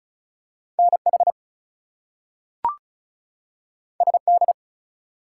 Morse code, heard as a keyed tone, about 36 wpm, 700 Hz. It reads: NH SD
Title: US States and Canadian Provinces 35wpm